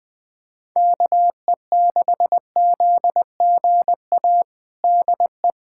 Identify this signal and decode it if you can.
Morse code, heard as a keyed tone, about 20 wpm, 700 Hz. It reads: KE6ZGA DE
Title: USA Callsign DE Callsign 20wpm